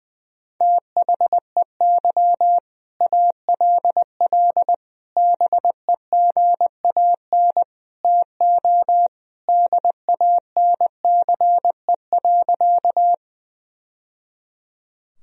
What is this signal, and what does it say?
Morse code, heard as a keyed tone, about 21 wpm, 700 Hz. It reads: THEY ALL BEGAN TO DANCE.
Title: Sentences from Top 1000 Words - Review Part 4 - 20wpm